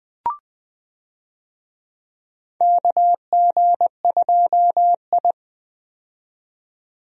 Morse code, heard as a keyed tone, about 20 wpm, 700 Hz. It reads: KG2I
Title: USA Call Signs - 4 Characters - 20wpm